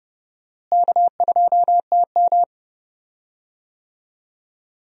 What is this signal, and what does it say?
Morse code, heard as a keyed tone, about 30 wpm, 700 Hz. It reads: K2TM
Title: USA Call Signs - 4 Characters - Repeated 2x - 30wpm